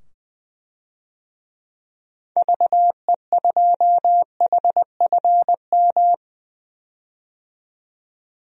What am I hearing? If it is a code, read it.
Morse code, heard as a keyed tone, about 20 wpm, 700 Hz. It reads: VE2HFM
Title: DXCC-100 Call Signs - 6 Characters - 20wpm